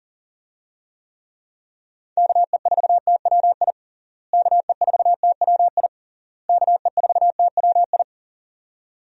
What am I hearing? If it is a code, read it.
Morse code, heard as a keyed tone, about 40 wpm, 700 Hz. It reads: KE4TWI KE4TWI KE4TWI
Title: USA Call Signs - 6 Characters - Repeated 3x - 40wpm